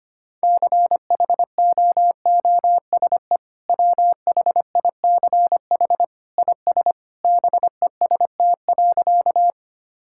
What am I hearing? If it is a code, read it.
Morse code, heard as a keyed tone, about 25 wpm, 700 Hz. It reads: CHOOSE WHICH IS BEST.
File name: practice / 25wpm / Sentences with 2-Word Phrases - 25wpm